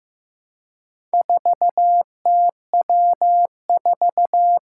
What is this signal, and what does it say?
Morse code, heard as a keyed tone, about 15 wpm, 700 Hz. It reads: 4TW4
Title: Four Challenging Letter-Number Combinations 15wpm